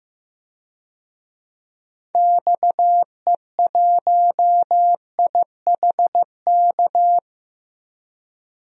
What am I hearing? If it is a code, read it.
Morse code, heard as a keyed tone, about 16 wpm, 700 Hz. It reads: XE1IHK
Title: DXCC-100 Call Signs - 6 Characters - 15wpm